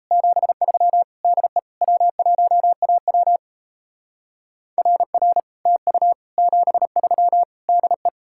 Morse code, heard as a keyed tone, about 36 wpm, 700 Hz. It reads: 73 DE W1AW RR TU 73 DE